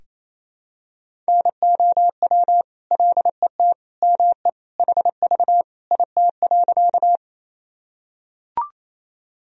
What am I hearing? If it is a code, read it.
Morse code, heard as a keyed tone, about 28 wpm, 700 Hz. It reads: NOW LET ME HV IT.
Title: Sentences from Top 300 Words and Abbreviations 28wpm